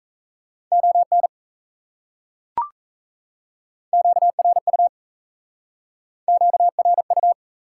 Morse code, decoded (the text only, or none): ON QRU QRU